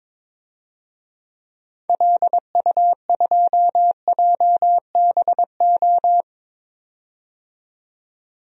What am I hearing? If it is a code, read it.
Morse code, heard as a keyed tone, about 22 wpm, 700 Hz. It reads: LU2JBO
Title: DXCC-100 Call Signs - 6 Characters - 22wpm